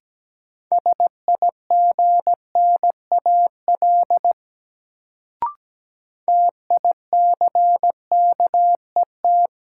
Morse code, heard as a keyed tone, about 17 wpm, 700 Hz. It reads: SIGNAL TICKET
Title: Sentences from Top 1400 Words - New Word Drill Alt - 17wpm